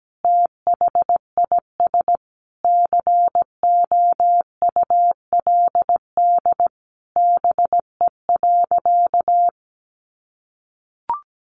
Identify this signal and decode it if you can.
Morse code, heard as a keyed tone, about 17 wpm, 700 Hz. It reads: THIS COULD BE.
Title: Sentences from Top 1400 Words 17wpm